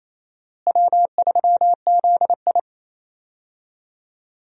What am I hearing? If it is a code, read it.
Morse code, heard as a keyed tone, about 28 wpm, 700 Hz. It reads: W3ZI
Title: USA Call Signs - 4 Characters - Repeated 2x - 28wpm